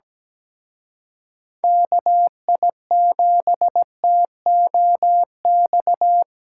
Morse code, heard as a keyed tone, about 20 wpm, 700 Hz. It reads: KI7TOX
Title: Speed Racing - USA 2x3 Call Signs 17wpm